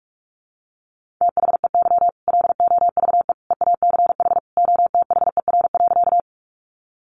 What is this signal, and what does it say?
Morse code, heard as a keyed tone, about 44 wpm, 700 Hz. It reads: THEY LOVE EACH OTHER.
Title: Easy Phrases 45wpm